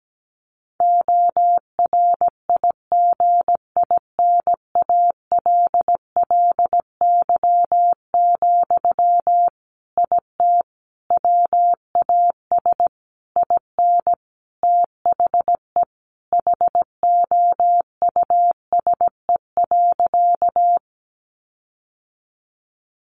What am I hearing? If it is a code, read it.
Morse code, heard as a keyed tone, about 17 wpm, 700 Hz. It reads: ORIGINALLY, IT WAS IN THE HOUSE.
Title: Sentences from Top 1500 Words 17wpm